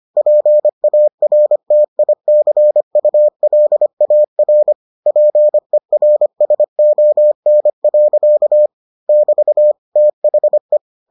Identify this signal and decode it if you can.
Morse code, heard as a keyed tone, about 25 wpm, 600 Hz. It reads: PARTICULAR PERSON. <BT> THE